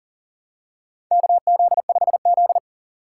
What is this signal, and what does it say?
Morse code, heard as a keyed tone, about 40 wpm, 700 Hz. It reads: KZ5Z
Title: K1USN SST QSO Practice - 40wpm